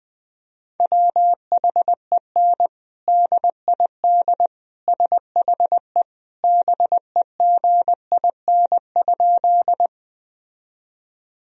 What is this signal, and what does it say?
Morse code, heard as a keyed tone, about 20 wpm, 700 Hz. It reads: WHEN DID SHE BEGIN?
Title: Sentences from Top 1000 Words - Review Part 1 - 20wpm